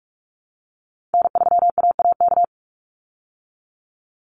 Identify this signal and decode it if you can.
Morse code, heard as a keyed tone, about 44 wpm, 700 Hz. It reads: N3AAK
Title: Call Signs Repeated 3x - 45wpm